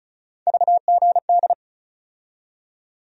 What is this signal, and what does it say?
Morse code, heard as a keyed tone, about 36 wpm, 700 Hz. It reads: VGD
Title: USA Call Sign Postfixes - 3 Letters 35wpm